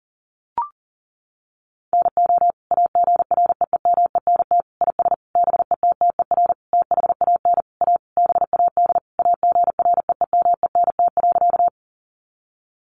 Morse code, heard as a keyed tone, about 40 wpm, 700 Hz. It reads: NO AGREEMENT IS BETTER THAN A BAD AGREEMENT.
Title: Sentences from Top 1500 Words 40wpm